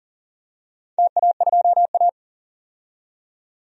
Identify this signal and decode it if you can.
Morse code, heard as a keyed tone, about 40 wpm, 700 Hz. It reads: TA2A
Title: DXCC-100 Call Signs - 4 Characters - 40wpm